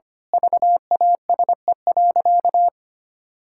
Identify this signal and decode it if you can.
Morse code, heard as a keyed tone, about 25 wpm, 700 Hz. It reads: VASE.
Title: Pangram Challenge v2 - 25wpm